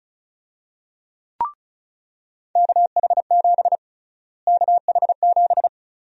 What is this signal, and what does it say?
Morse code, heard as a keyed tone, about 36 wpm, 700 Hz. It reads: KH7 KH7
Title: USA Call Sign Prefixes - Repeated 2x - 35wpm